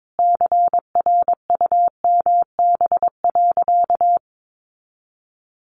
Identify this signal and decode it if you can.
Morse code, heard as a keyed tone, about 22 wpm, 700 Hz. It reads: CRUMB.
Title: Easy Phrases 22wpm